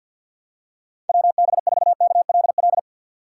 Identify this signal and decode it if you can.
Morse code, heard as a keyed tone, about 54 wpm, 700 Hz. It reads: WB4KLL
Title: USA Call Signs - 6 Characters - 8x Character Spacing - 50wpm